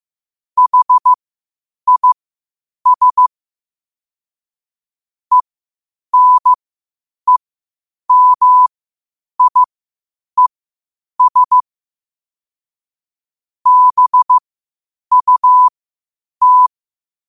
Morse code, heard as a keyed tone, about 15 wpm, spaced as for 8 wpm, 1000 Hz. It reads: HIS ENEMIES BUT
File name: qotdfiles / 8wpm